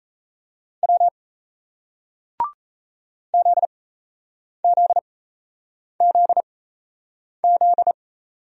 Morse code, heard as a keyed tone, about 36 wpm, spaced as for 8 wpm, 700 Hz. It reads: W ZZZZ